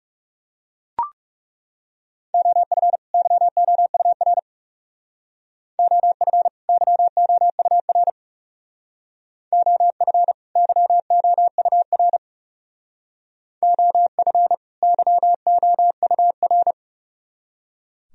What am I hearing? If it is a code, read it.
Morse code, heard as a keyed tone, about 39 wpm, 700 Hz. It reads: OF YOUR OF YOUR OF YOUR OF YOUR